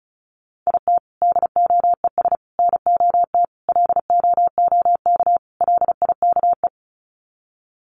Morse code, heard as a keyed tone, about 34 wpm, 700 Hz. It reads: IT DOES NOT LOOK LIKE
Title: Sets of 5 Words Encore2 35wpm